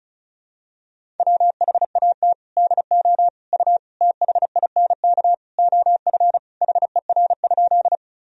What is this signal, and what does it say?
Morse code, heard as a keyed tone, about 34 wpm, 690 Hz. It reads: WHAT DO U THINK OF HER?
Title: Sentences from Top 200 Words and Abbreviations 35wpm